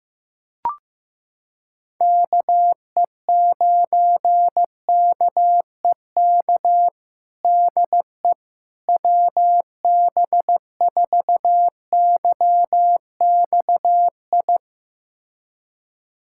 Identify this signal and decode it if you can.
Morse code, heard as a keyed tone, about 15 wpm, 700 Hz. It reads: KE9KEK DE WB4YXI
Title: USA Callsign DE Callsign - 6 Characters - 15wpm